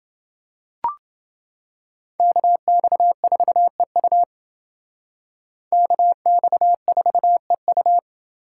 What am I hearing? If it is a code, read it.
Morse code, heard as a keyed tone, about 28 wpm, 700 Hz. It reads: KX4EU KX4EU